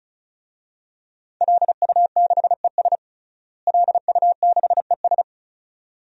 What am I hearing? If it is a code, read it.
Morse code, heard as a keyed tone, about 36 wpm, 700 Hz. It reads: LU6ES LU6ES
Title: DXCC-100 Call Signs - 5 Characters - Repeated 2x - 35wpm